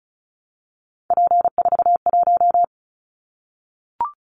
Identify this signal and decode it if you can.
Morse code, heard as a keyed tone, about 36 wpm, 700 Hz. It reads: P41
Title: Callsign Prefixes for Top 100 DXCC - 35wpm